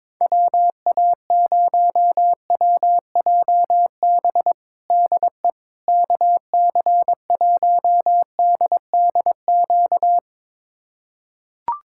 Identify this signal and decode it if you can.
Morse code, heard as a keyed tone, about 21 wpm, 700 Hz. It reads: WA0WJB DE KC1DDQ
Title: USA Callsign DE Callsign - 6 Characters - 22wpm